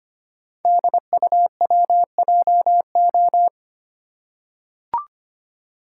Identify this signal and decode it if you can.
Morse code, heard as a keyed tone, about 25 wpm, 700 Hz. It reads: DUWJO
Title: Code Groups - 5 Letter:Numbers - 25wpm